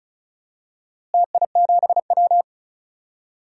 Morse code, heard as a keyed tone, about 36 wpm, 700 Hz. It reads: TI7W
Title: DXCC-100 Call Signs - 4 Characters - Repeated 2x - 35wpm